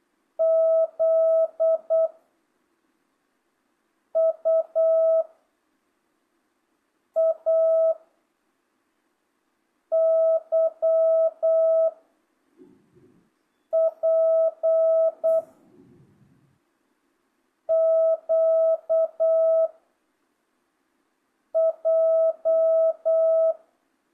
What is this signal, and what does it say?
Morse code, heard as a keyed tone, about 8 wpm, 660 Hz. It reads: Z U A Y P Q J
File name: jh8cbh / 2025cw